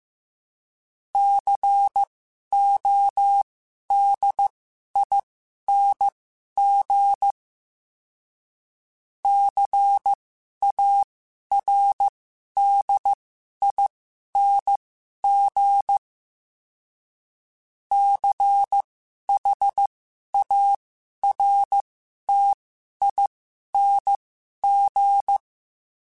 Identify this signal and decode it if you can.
Morse code, heard as a keyed tone, about 15 wpm, spaced as for 11 wpm, 800 Hz. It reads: CODING CARDING CHARTING
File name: Special INGs-mp3s / Special INGs-mp3s / Sngl Ing @10